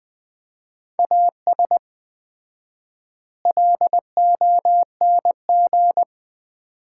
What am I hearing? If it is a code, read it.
Morse code, heard as a keyed tone, about 20 wpm, 700 Hz. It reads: AS LONG
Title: 4x Word-Spacing - Sets of 4 Words - 20wpm